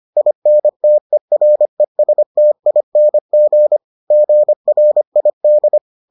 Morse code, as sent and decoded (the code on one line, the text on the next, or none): .. -. - . .-. ... - .. -. --. / --. .-. .. -..
INTERESTING GRID